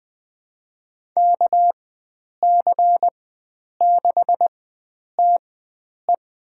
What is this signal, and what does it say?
Morse code, heard as a keyed tone, about 20 wpm, spaced as for 9 wpm, 700 Hz. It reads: KC6TE
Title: USA Call Signs - 4x Character Spacing - 20wpm